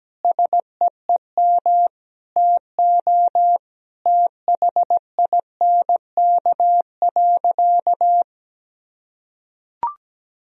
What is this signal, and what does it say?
Morse code, heard as a keyed tone, about 17 wpm, 700 Hz. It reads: SEEM TO THINK.
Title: Sentences from Top 1000 Words - Review Part 1 - 17wpm